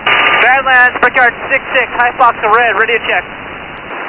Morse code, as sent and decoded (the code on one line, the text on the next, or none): none